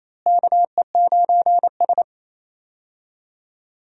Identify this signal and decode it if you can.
Morse code, heard as a keyed tone, about 28 wpm, 700 Hz. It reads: KE9S
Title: USA Call Signs - 4 Characters - Repeated 2x - 28wpm